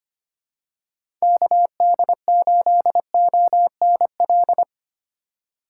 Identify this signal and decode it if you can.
Morse code, heard as a keyed tone, about 25 wpm, 700 Hz. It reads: KD8ONL
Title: USA Call Signs - 6 Characters - 2x Character Spacing - 25wpm